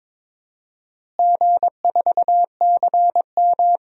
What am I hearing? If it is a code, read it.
Morse code, heard as a keyed tone, about 22 wpm, 700 Hz. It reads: G4CM
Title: DXCC-100 Call Signs - 4 Characters - Repeated 3x - 22wpm